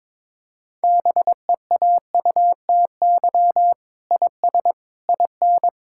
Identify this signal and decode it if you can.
Morse code, heard as a keyed tone, about 22 wpm, 700 Hz. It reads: BEAUTY IS IN
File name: practice / 22wpm / Sentences with 8-Word Phrases - 22wpm